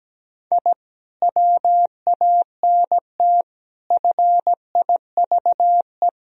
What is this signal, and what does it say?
Morse code, heard as a keyed tone, about 17 wpm, 700 Hz. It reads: I WANT FIVE
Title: Sentences from Top 1000 Words - Review Part 2 - 17wpm